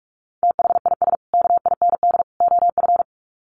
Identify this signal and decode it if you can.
Morse code, heard as a keyed tone, about 44 wpm, 700 Hz. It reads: THIS KIND OF
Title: Sets of 3 Words 45wpm